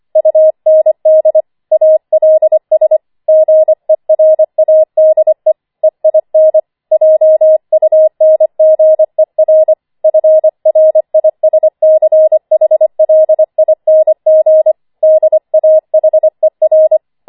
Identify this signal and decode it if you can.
Morse code, heard as a keyed tone, about 25 wpm, 630 Hz. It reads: UND ALS GERADE EIN JUNGER FRISCHLING DAHER
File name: 25wpm